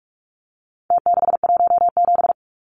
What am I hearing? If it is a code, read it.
Morse code, heard as a keyed tone, about 44 wpm, 700 Hz. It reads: T617
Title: Numbers - Contest Style - 4-Digit 45wpm